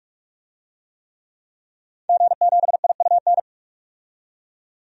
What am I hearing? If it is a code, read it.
Morse code, heard as a keyed tone, about 46 wpm, 700 Hz. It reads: G7IUN
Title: DXCC-100 Call Signs - 5 Characters - 45wpm